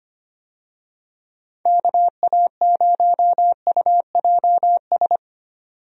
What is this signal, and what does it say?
Morse code, heard as a keyed tone, about 25 wpm, 700 Hz. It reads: KA0UJS
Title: USA Call Signs - 6 Characters - 25wpm